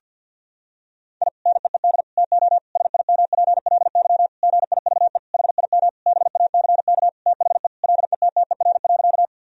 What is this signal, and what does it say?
Morse code, heard as a keyed tone, about 50 wpm, 700 Hz. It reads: I NEED TO SIMPLY GIVE HIM BACK THE LETTER.
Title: Sentences from Top 1000 Words - Review Part 4 - 50wpm